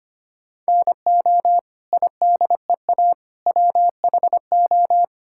Morse code, heard as a keyed tone, about 25 wpm, 700 Hz. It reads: NO IDEA WHO
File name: Sets of 9 Words - 25wpm